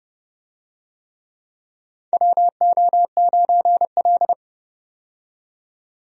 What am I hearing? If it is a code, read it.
Morse code, heard as a keyed tone, about 30 wpm, 700 Hz. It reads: WO9L
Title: USA Call Signs - 4 Characters - 30wpm